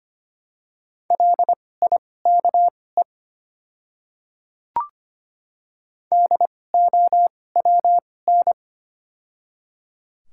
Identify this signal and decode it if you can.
Morse code, heard as a keyed tone, about 25 wpm, spaced as for 18 wpm, 700 Hz. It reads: LIKE DOWN
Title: 2x Character Spacing - Top 100 Words - 25wpm